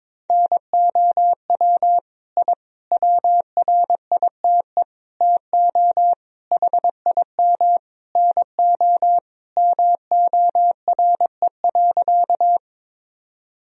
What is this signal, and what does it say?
Morse code, heard as a keyed tone, about 22 wpm, 700 Hz. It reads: NOW I WRITE TO HIM NO MORE.
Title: Sentences from Top 1000 Words - Review Part 1 - 22wpm